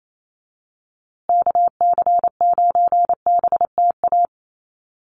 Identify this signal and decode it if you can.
Morse code, heard as a keyed tone, about 28 wpm, 700 Hz. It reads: KC9BTA
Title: USA Call Signs Repeated 2x 28wpm